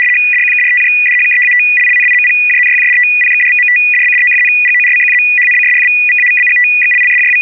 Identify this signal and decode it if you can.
RTTY, about 75 baud. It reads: DQKRTJDHKEATLLNIMPJNYJDTPXDVMDBDOFNGORCQZHNCSFVVAVBGXLLSJW
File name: RUS ITA-2-ARQ